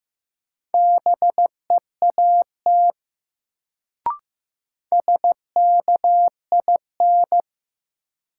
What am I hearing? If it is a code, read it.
Morse code, heard as a keyed tone, about 15 wpm, 700 Hz. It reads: BEAT SKIN
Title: Sentences from Top 1300 Words - New Word Drill - 15wpm